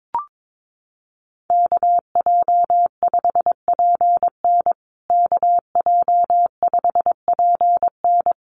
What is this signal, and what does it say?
Morse code, heard as a keyed tone, about 22 wpm, 700 Hz. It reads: KJ5PN KJ5PN
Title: USA Call Signs Repeated 2x 22wpm